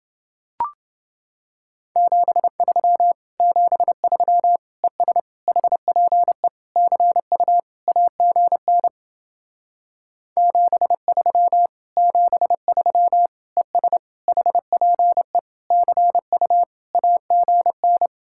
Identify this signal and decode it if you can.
Morse code, heard as a keyed tone, about 30 wpm, 700 Hz. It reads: 73 73 ES HPE CU AGN 73 73 ES HPE CU AGN